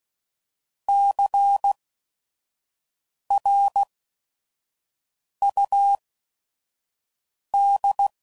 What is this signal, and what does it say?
Morse code, heard as a keyed tone, about 16 wpm, spaced as for 5 wpm, 800 Hz. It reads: CRUD
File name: Prefixes / By First Letter / Cs